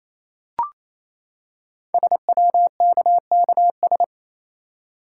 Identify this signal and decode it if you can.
Morse code, heard as a keyed tone, about 27 wpm, 700 Hz. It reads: SWKKS
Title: Code Groups - 5 Letters - 28wpm